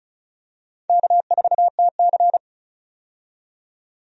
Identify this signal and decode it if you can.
Morse code, heard as a keyed tone, about 36 wpm, 700 Hz. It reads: K4TC
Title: USA Call Signs - 4 Characters - 35wpm